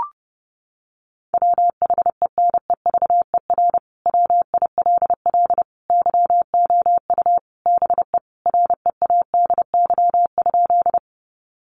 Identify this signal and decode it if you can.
Morse code, heard as a keyed tone, about 30 wpm, 700 Hz. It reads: WHENEVER WILL YOU BE READY?
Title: Sentences from Top 1200 Words 30wpm